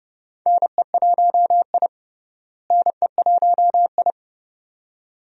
Morse code, heard as a keyed tone, about 30 wpm, 700 Hz. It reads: NE1I NE1I
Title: USA Call Signs - 4 Characters - Repeated 2x - 30wpm